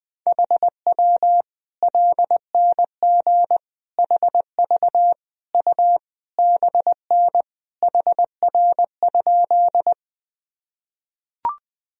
Morse code, heard as a keyed tone, about 20 wpm, 700 Hz. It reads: HW LNG HV U BN HR?
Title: Sentences from Top 200 Words and Abbreviations 20wpm